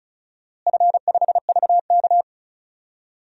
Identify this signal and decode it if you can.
Morse code, heard as a keyed tone, about 34 wpm, 700 Hz. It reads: F5VK
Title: DXCC-100 Call Signs - 4 Characters - Repeated 2x - 35wpm